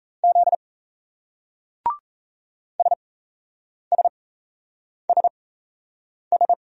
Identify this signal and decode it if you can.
Morse code, heard as a keyed tone, about 38 wpm, spaced as for 8 wpm, 700 Hz. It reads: Z SSSS